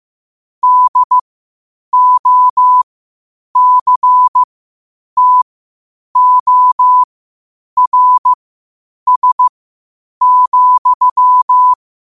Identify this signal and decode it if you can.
Morse code, heard as a keyed tone, about 15 wpm, spaced as for 8 wpm, 1000 Hz. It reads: DOCTORS,
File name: qotdfiles / 8wpm